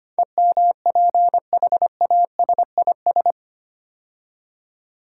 Morse code, heard as a keyed tone, about 25 wpm, 700 Hz. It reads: EMPHASIS
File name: 8 Letter Words - First 3 Letters Given - 25wpm